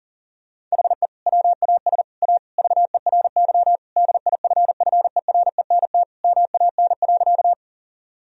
Morse code, heard as a keyed tone, about 37 wpm, 700 Hz. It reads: HE WAS A VERY DIFFERENT MAN.
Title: Sentences from Top 1000 Words - Review Part 3 - 40wpm